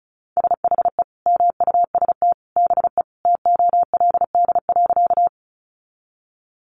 Morse code, heard as a keyed tone, about 34 wpm, 700 Hz. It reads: SHE MUST BE TOLD.
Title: Sentences from Top 400 Words 35wpm